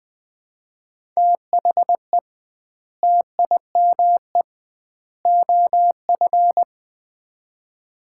Morse code, heard as a keyed tone, about 21 wpm, 700 Hz. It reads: THE TIME OF
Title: 2x Word-Spacing - Sets of 3 Words - 20wpm